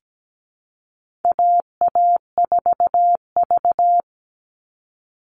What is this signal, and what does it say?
Morse code, heard as a keyed tone, about 17 wpm, 700 Hz. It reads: AA4V
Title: All Cut Numbers - 4-Digit 17wpm